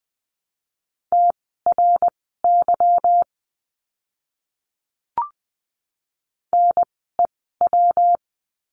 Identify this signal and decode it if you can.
Morse code, heard as a keyed tone, about 20 wpm, spaced as for 14 wpm, 700 Hz. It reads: TRY NEW
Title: 244 Lesson 39f - Letter z - Words 20wpm